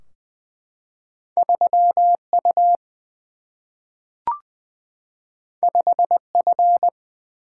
Morse code, heard as a keyed tone, about 20 wpm, 700 Hz. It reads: 3 U 5 F
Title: Single-Letter Single-Number 20wpm